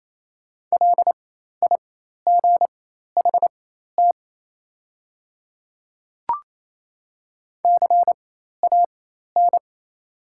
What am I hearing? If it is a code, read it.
Morse code, heard as a keyed tone, about 28 wpm, spaced as for 13 wpm, 700 Hz. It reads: LIGHT CAN
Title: 4x Character Spacing - Top 300 Words - 28wpm